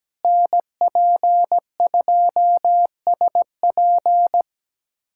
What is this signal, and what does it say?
Morse code, heard as a keyed tone, about 17 wpm, 700 Hz. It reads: NP2SP
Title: USA Callsign DE Callsign - 5 Characters - 17wpm